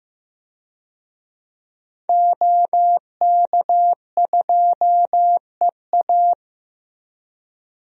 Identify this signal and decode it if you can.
Morse code, heard as a keyed tone, about 16 wpm, 700 Hz. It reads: OK2EA
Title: DXCC-100 Call Signs - 5 Characters - Repeated 2x - 15wpm